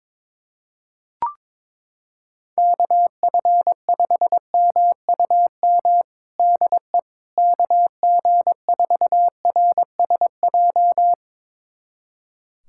Morse code, heard as a keyed tone, about 22 wpm, 700 Hz. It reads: KF5MUM DE KG4RSJ
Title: USA Callsign DE Callsign - 6 Characters - 22wpm